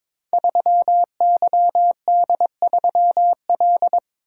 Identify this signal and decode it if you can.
Morse code, heard as a keyed tone, about 22 wpm, 700 Hz. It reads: 3YD3L
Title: Code Groups - 5 Letter:Numbers - 22wpm